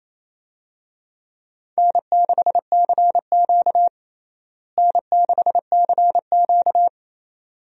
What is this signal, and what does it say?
Morse code, heard as a keyed tone, about 27 wpm, 700 Hz. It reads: N6CQ N6CQ
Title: USA Call Signs - 4 Characters - Repeated 2x - 28wpm